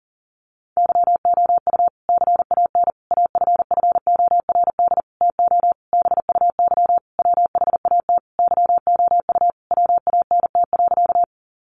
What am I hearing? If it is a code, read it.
Morse code, heard as a keyed tone, about 37 wpm, 700 Hz. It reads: YOU CAN AFFORD TO BUY WHAT YOU WANT.
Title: Sentences from Top 1100 Words 40wpm